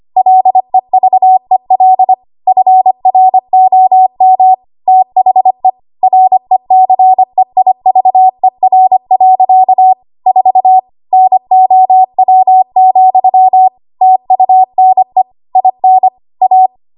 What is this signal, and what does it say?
Morse code, heard as a keyed tone, about 25 wpm, 750 Hz. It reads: LEVEL FROM THE RECEIVER. 4 NOW, TUNE IN A
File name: Archive / 25 WPM / 25WPM